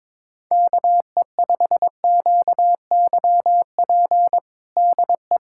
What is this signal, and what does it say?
Morse code, heard as a keyed tone, about 21 wpm, 700 Hz. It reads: KE5QYP DE
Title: USA Callsign DE Callsign - 6 Characters - 22wpm